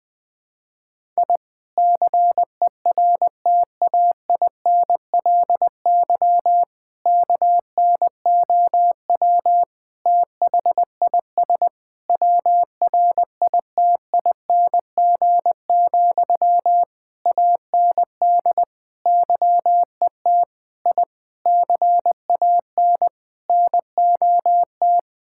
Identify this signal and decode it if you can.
Morse code, heard as a keyed tone, about 20 wpm, 700 Hz. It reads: I CERTAINLY KNOW THIS WRITING, AND YET I CAN NOT
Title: Sentences from Top 1000 Words - Review Part 6 - 20wpm